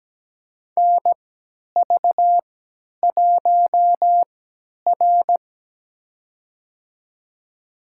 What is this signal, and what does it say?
Morse code, heard as a keyed tone, about 17 wpm, spaced as for 10 wpm, 700 Hz. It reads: NV1R
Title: USA Call Signs - 4 Characters - 3x Character Spacing - 17wpm